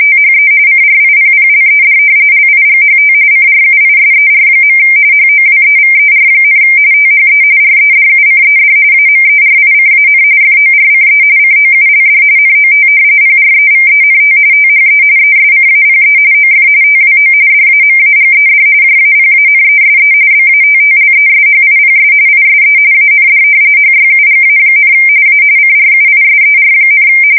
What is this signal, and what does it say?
RTTY, about 45.45 baud. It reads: THE CRIPPLED LIGHT CRUISER. HOWEVER, THE REDUCED AMERICAN FORCE COULD NOT PREVENT THE ESCAPE OF TWO DAMAGED JAPANESE BATTLESHIPS ISE AND HYUGA, A CRUISER,
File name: fsk170 / battleshipnj